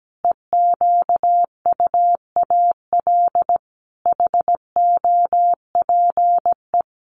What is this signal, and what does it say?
Morse code, heard as a keyed tone, about 17 wpm, 700 Hz. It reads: EQUAL HOPE
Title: Sentences from Top 1400 Words 17wpm